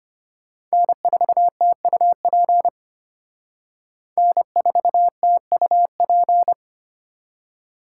Morse code, heard as a keyed tone, about 30 wpm, 700 Hz. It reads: N4TUP N4TUP
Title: Speed Racing - USA 1x3 Call Signs 25wpm